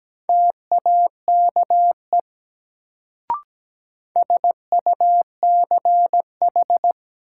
Morse code, heard as a keyed tone, about 17 wpm, 700 Hz. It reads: TAKE SUCH
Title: Sets of 2 Words - New Word Drill - 17wpm